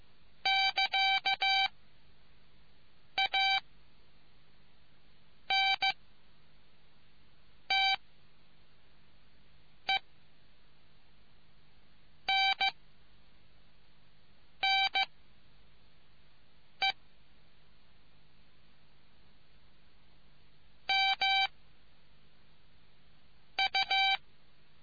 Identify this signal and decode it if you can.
Morse code, heard as a keyed tone, about 15 wpm, 780 Hz. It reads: <KA> A N T E N N E M U